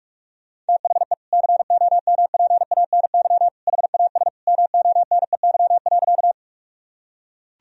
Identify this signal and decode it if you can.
Morse code, heard as a keyed tone, about 41 wpm, 700 Hz. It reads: THE COMPANY HAS MONEY.
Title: Sentences from Top 1000 Words - Review Part 3 - 45wpm